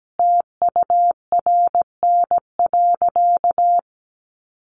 Morse code, heard as a keyed tone, about 17 wpm, 700 Hz. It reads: TURN.
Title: Sentences from Top 600 Words 17wpm